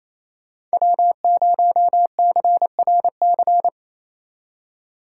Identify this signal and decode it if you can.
Morse code, heard as a keyed tone, about 37 wpm, 700 Hz. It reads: W0CRC